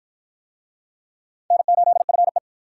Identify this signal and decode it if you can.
Morse code, heard as a keyed tone, about 47 wpm, 700 Hz. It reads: N8FE